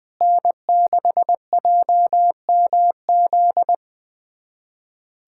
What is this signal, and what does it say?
Morse code, heard as a keyed tone, about 20 wpm, 700 Hz. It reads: N6JMZ
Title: USA Callsign DE Callsign 20wpm